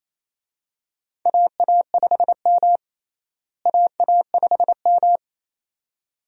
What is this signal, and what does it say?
Morse code, heard as a keyed tone, about 28 wpm, 700 Hz. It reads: AA5M AA5M
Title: USA Call Signs - 4 Characters - Repeated 2x - 28wpm